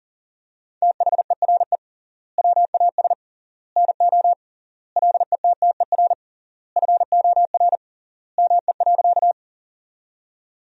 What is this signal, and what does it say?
Morse code, heard as a keyed tone, about 40 wpm, 700 Hz. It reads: THERE WAS NO LETTER FOR ME.
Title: 3x Word-Spacing - Sentences from Top 300 Words - 40wpm